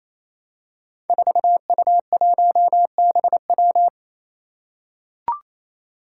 Morse code, heard as a keyed tone, about 28 wpm, 700 Hz. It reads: 4U1BW
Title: Five Challenging Letter-Number Combinations 28wpm